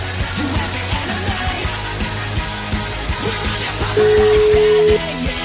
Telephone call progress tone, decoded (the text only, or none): ringback tone